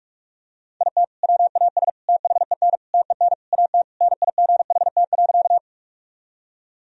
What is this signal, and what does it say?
Morse code, heard as a keyed tone, about 46 wpm, 700 Hz. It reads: IT WAS THEN TEN AT NIGHT.
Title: Sentences from Top 1000 Words - Review Part 2 - 45wpm